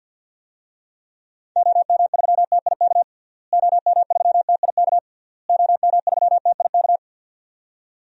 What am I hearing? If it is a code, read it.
Morse code, heard as a keyed tone, about 52 wpm, 700 Hz. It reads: OM3TIK OM3TIK OM3TIK
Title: DXCC-100 Call Signs - 6 Characters - Repeated 3x - 50wpm